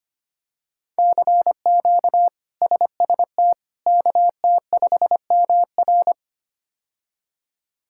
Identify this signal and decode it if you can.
Morse code, heard as a keyed tone, about 25 wpm, 700 Hz. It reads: CQ SST KT5MR
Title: K1USN SST QSO Practice - 25wpm